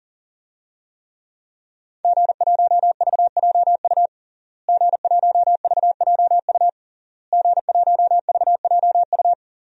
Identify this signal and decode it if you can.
Morse code, heard as a keyed tone, about 40 wpm, 700 Hz. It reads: G1VJU G1VJU G1VJU
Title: DXCC-100 Call Signs - 5 Characters - Repeated 3x - 40wpm